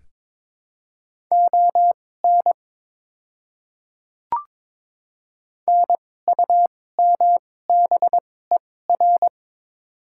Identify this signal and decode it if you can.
Morse code, heard as a keyed tone, about 22 wpm, spaced as for 16 wpm, 700 Hz. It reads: ON NUMBER
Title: 2x Character Spacing - Top 100 Words - 22wpm